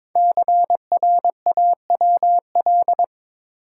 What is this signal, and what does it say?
Morse code, heard as a keyed tone, about 22 wpm, 700 Hz. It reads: CRAWL
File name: practice / 22wpm / 5 Letter Words - First 3 Letters Given - 22wpm